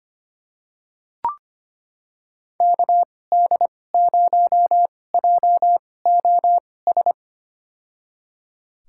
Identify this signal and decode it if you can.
Morse code, heard as a keyed tone, about 25 wpm, spaced as for 18 wpm, 700 Hz. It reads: KD0JOS
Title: USA Call Signs - 2x Character Spacing - 25wpm